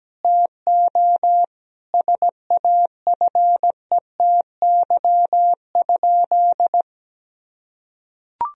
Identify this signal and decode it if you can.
Morse code, heard as a keyed tone, about 17 wpm, 700 Hz. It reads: TO SAFETY?
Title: Sentences from Top 1000 Words - Review Part 4 - 17wpm